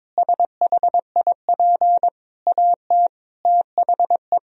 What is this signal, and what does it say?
Morse code, heard as a keyed tone, about 22 wpm, 700 Hz. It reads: SHIP AT THE
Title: Sentences from Top 1000 Words - Review Part 2 - 22wpm